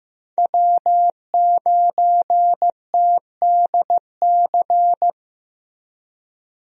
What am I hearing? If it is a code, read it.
Morse code, heard as a keyed tone, about 15 wpm, 700 Hz. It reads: W9TDC
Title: USA Call Signs - 5 Characters - Repeated 2x - 15wpm